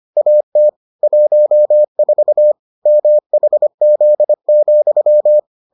Morse code, heard as a keyed tone, about 25 wpm, 600 Hz. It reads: AT 14 MHZ,